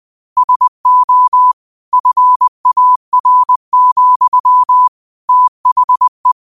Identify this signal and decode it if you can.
Morse code, heard as a keyed tone, about 20 wpm, 1000 Hz. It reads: SO FAR, THE
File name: qotdfiles / 20wpm